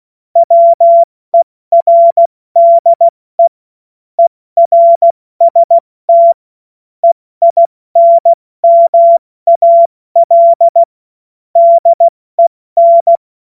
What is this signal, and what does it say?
Morse code, heard as a keyed tone, about 16 wpm, 680 Hz. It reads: WERDE ERST EINMAL DEN
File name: Morsekurs / Witze / Witz 28. Dezember 2021, 70 F80